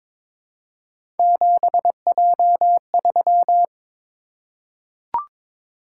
Morse code, heard as a keyed tone, about 22 wpm, 700 Hz. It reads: 7J3
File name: practice / 22wpm / Three Letter-Number Combinations - 22wpm